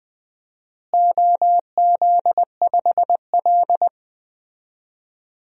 Morse code, heard as a keyed tone, about 20 wpm, 700 Hz. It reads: OZ5L
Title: DXCC-100 Call Signs - 4 Characters - Repeated 2x - 20wpm